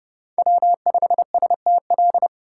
Morse code, heard as a keyed tone, about 32 wpm, 700 Hz. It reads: W5STL
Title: USA Call Signs - 5 Characters - 4x Character Spacing - 30wpm